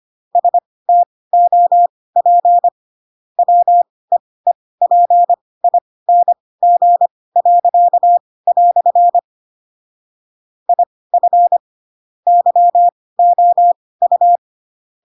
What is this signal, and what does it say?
Morse code, heard as a keyed tone, about 25 wpm, spaced as for 18 wpm, 700 Hz. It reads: STOP WEEPING." IF YOU
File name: Pinocchio / Pinocchio ch7 12 25 ik1hge